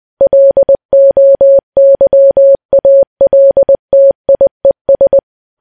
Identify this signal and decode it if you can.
Morse code, heard as a keyed tone, about 20 wpm, 550 Hz. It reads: LOYALTIES
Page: news-headlines-in-morse-code-at-20-wpm